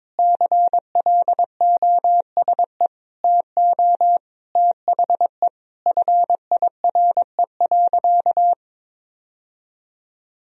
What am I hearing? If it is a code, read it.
Morse code, heard as a keyed tone, about 22 wpm, 700 Hz. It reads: CLOSE TO THE FIRE.
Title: Sentences from Top 1000 Words - Review Part 1 - 22wpm